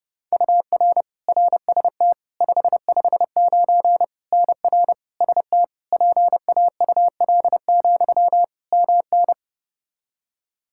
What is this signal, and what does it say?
Morse code, heard as a keyed tone, about 32 wpm, 700 Hz. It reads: UR RST 559 NR ST PAUL, MN
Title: Practice QSO - Exchange 1 - Beginning - 30wpm